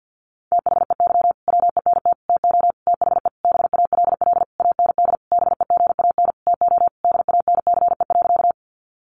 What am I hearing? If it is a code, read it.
Morse code, heard as a keyed tone, about 50 wpm, 700 Hz. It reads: THEY WENT TO THE BALL AND BEGAN TO DANCE.
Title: Sentences from Top 900 Words 50wpm